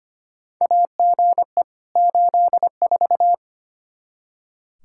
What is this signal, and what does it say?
Morse code, heard as a keyed tone, about 25 wpm, 700 Hz. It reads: AGE 84
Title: Practice QSO - Exchange 3 - More Info - 25wpm